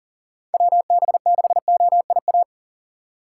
Morse code, heard as a keyed tone, about 40 wpm, 700 Hz. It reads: WB6OIA
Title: USA Call Signs - 2x Character Spacing - 40wpm